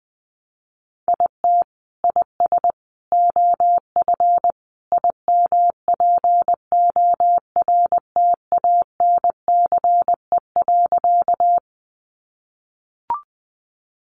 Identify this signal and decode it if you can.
Morse code, heard as a keyed tone, about 20 wpm, 700 Hz. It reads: IT IS OF IMPORTANCE.
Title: Sentences from Top 1200 Words 20wpm